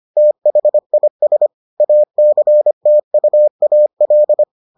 Morse code, heard as a keyed tone, about 25 wpm, 600 Hz. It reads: THIS ACTUAL